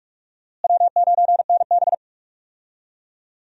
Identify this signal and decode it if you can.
Morse code, heard as a keyed tone, about 46 wpm, 700 Hz. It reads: W9NB
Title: USA Call Signs - 4 Characters - Repeated 2x - 45wpm